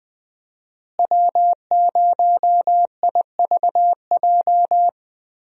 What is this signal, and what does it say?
Morse code, heard as a keyed tone, about 20 wpm, 700 Hz. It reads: W0IVJ
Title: USA Call Signs - 5 Characters - Repeated 2x - 20wpm